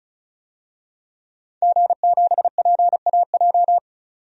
Morse code, heard as a keyed tone, about 36 wpm, 700 Hz. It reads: G7PAJ
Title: DXCC-100 Call Signs - 5 Characters - 35wpm